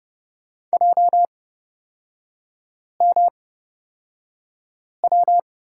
Morse code, heard as a keyed tone, about 30 wpm, spaced as for 5 wpm, 700 Hz. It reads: JMW
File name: Lesson 36e - BT - Three Characters 30wpm